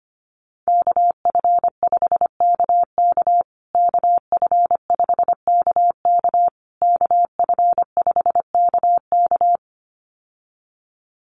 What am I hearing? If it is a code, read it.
Morse code, heard as a keyed tone, about 25 wpm, 700 Hz. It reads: KF5KK KF5KK KF5KK
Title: USA Call Signs Repeated 3x 25wpm